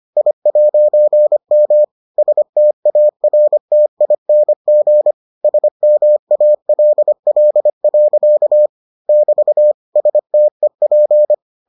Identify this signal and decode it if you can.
Morse code, heard as a keyed tone, about 25 wpm, 600 Hz. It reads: I'M STARTING SMALL. <BT> STEP